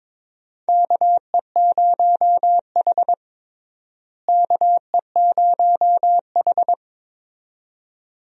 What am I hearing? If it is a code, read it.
Morse code, heard as a keyed tone, about 23 wpm, 700 Hz. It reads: KE0H KE0H
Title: USA Call Signs - 4 Characters - Repeated 2x - 22wpm